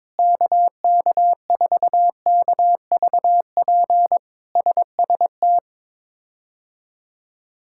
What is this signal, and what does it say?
Morse code, heard as a keyed tone, about 22 wpm, 700 Hz. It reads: KK4KVP SST
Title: K1USN SST QSO Practice - 22wpm